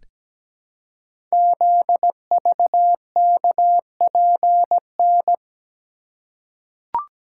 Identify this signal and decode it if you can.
Morse code, heard as a keyed tone, about 17 wpm, 700 Hz. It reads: ZVKPN
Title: Code Groups - 5 Letters - 17wpm